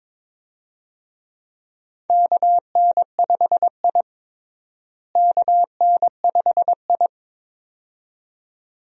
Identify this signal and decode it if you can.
Morse code, heard as a keyed tone, about 22 wpm, 700 Hz. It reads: KN5I KN5I
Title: USA Call Signs - 4 Characters - Repeated 2x - 22wpm